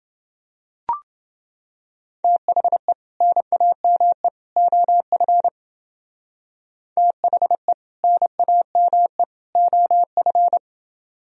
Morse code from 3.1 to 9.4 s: NAME OF THE NAME

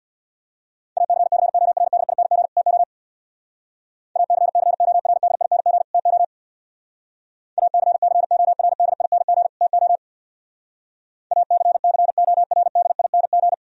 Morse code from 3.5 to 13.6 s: ACCORDING TO ACCORDING TO ACCORDING

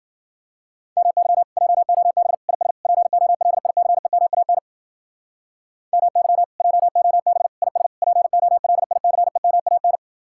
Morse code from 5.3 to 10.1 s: MY JOB IS POLICEMAN

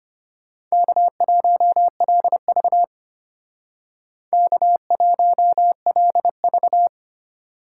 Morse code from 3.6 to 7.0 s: K1LV

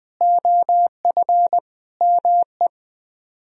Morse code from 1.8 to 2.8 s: ME

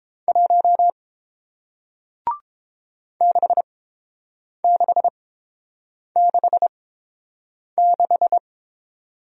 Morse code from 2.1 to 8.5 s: 6666

